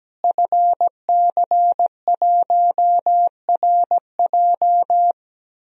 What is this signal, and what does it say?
Morse code, heard as a keyed tone, about 17 wpm, 700 Hz. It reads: FC1RJ
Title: DXCC-100 Call Signs - 5 Characters - Repeated 2x - 17wpm